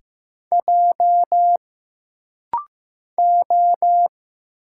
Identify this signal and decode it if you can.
Morse code, heard as a keyed tone, about 15 wpm, 700 Hz. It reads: J O
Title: Single Letter-Number Alt - Rapid-Fire 15wpm